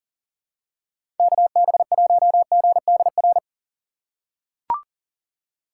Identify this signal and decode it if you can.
Morse code, heard as a keyed tone, about 37 wpm, 700 Hz. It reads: KB1GDR
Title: USA Call Signs - 6 Characters - 3x Character Spacing - 40wpm